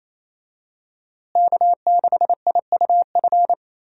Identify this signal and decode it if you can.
Morse code, heard as a keyed tone, about 28 wpm, 700 Hz. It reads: K6IUF
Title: USA Call Signs - 5 Characters - Repeated 2x - 28wpm